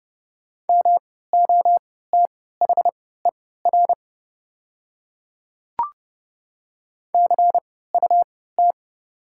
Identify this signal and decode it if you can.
Morse code, heard as a keyed tone, about 30 wpm, spaced as for 17 wpm, 690 Hz. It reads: MOTHER CUT
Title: 3x Character Spacing - Top 300 Words - 30wpm